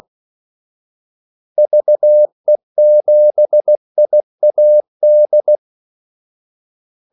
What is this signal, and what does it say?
Morse code, heard as a keyed tone, about 16 wpm, 600 Hz. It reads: VE7IAD
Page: Kaldesignaler 16 wpm